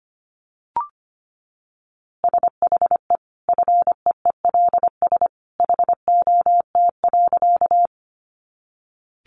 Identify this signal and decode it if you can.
Morse code, heard as a keyed tone, about 25 wpm, 700 Hz. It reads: SHE FEELS HOT.
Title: Easy Phrases 25wpm